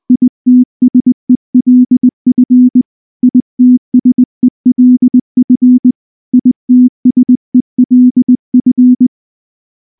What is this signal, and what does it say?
Morse code, heard as a keyed tone, about 19 wpm, 250 Hz. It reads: ITSELF ITSELF ITSELF
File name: morse-training-20WPM